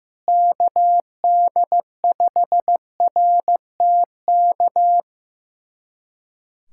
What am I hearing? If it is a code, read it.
Morse code, heard as a keyed tone, about 15 wpm, 700 Hz. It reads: KD5RTK
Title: USA Callsign DE Callsign - 6 Characters - 15wpm